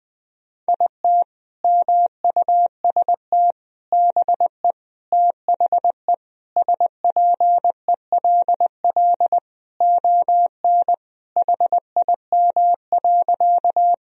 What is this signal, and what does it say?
Morse code, heard as a keyed tone, about 20 wpm, 700 Hz. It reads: IT MUST BE THE SPELL ON HIM.
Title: Sentences from Top 1000 Words - Review Part 1 - 20wpm